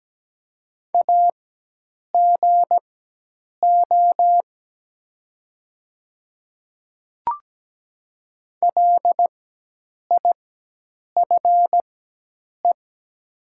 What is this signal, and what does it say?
Morse code, heard as a keyed tone, about 17 wpm, spaced as for 8 wpm, 700 Hz. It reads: AGO LIFE
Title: 4x Character Spacing - Top 500 Words - 17wpm